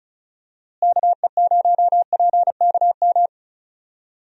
Morse code, heard as a keyed tone, about 34 wpm, 700 Hz. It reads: KE0PKM
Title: USA Call Signs - 6 Characters - Repeated 3x - 35wpm